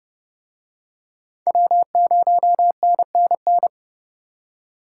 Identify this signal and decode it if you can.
Morse code, heard as a keyed tone, about 30 wpm, 700 Hz. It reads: W0NNN